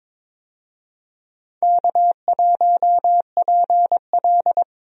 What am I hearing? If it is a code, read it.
Morse code, heard as a keyed tone, about 22 wpm, 700 Hz. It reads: K1PL
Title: USA Call Signs - 4 Characters - Repeated 2x - 22wpm